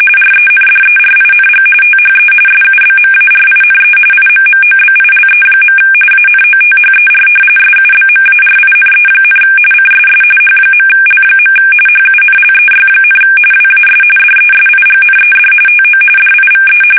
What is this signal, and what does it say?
RTTY, about 45.45 baud. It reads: RENDERING THE SUBMARINE IMMOBILE. ANOTHER SUBMARINE, CSS H.L. HUNLEY, WAS DESIGNED TO DIVE
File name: fsk850 / usnavy3